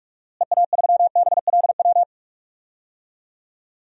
Morse code, heard as a keyed tone, about 46 wpm, 700 Hz. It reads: EA3BLW
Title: DXCC-100 Call Signs - 6 Characters - Repeated 3x - 45wpm